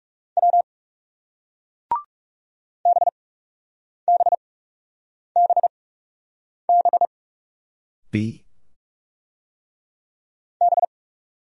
Morse code, heard as a keyed tone, about 41 wpm, spaced as for 8 wpm, 700 Hz. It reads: W BBBB B